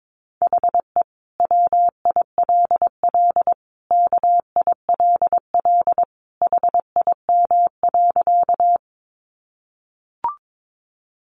Morse code, heard as a keyed tone, about 22 wpm, 700 Hz. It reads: HE WILL KILL HIM.
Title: Sentences from Top 1400 Words 22wpm